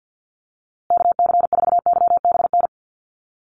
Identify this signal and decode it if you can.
Morse code, heard as a keyed tone, about 48 wpm, 700 Hz. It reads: KC4YBN
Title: USA Call Signs Repeated 3x 50wpm